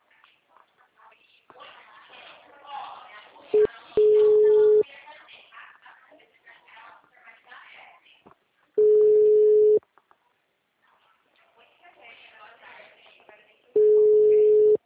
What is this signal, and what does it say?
Ringback tone (Europe, 425 Hz), with speech.
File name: out- -2020- - - .28